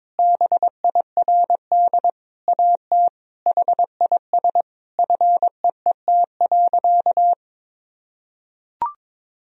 Morse code, heard as a keyed tone, about 22 wpm, 700 Hz. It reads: BIRD AT HIS FEET.
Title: Sentences from Top 1000 Words - Review Part 4 - 22wpm